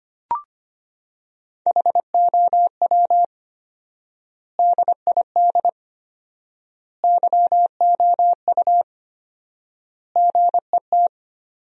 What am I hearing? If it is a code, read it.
Morse code, heard as a keyed tone, about 25 wpm, 700 Hz. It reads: HOW DID YOU GET
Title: 4x Word-Spacing - Sentences from Top 200 Words - 25wpm